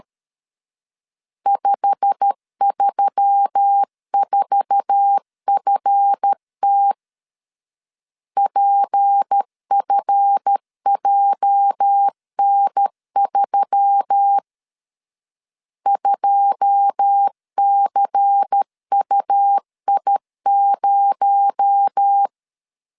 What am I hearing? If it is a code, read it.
Morse code, heard as a keyed tone, about 13 wpm, 800 Hz. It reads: 534FT PFJN3 2CUI0
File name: LEZIONI / 9 2 12wpm 2301